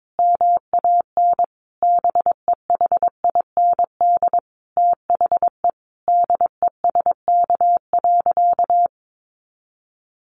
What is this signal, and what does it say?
Morse code, heard as a keyed tone, about 22 wpm, 700 Hz. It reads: MAN BEHIND THE DESK.
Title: A Sound Of Thunder 22wpm